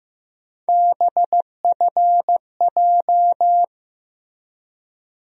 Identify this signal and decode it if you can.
Morse code, heard as a keyed tone, about 15 wpm, 700 Hz. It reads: BFJ
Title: USA Call Sign Postfixes - 3 Letters 15wpm